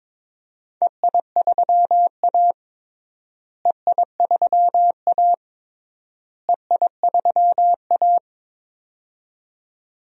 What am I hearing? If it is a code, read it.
Morse code, heard as a keyed tone, about 22 wpm, 700 Hz. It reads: EI3A EI3A EI3A
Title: DXCC-100 Call Signs - 4 Characters - Repeated 3x - 22wpm